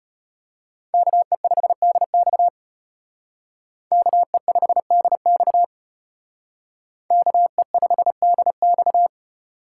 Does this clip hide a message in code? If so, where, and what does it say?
Yes: Morse code from 0.9 to 9.1 s, heard as a keyed tone, about 32 wpm, 700 Hz. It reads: KE5DX KE5DX KE5DX